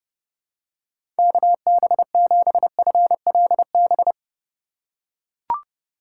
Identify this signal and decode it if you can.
Morse code, heard as a keyed tone, about 30 wpm, 700 Hz. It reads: KB7FLB
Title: USA Call Signs - 6 Characters - 4x Character Spacing - 30wpm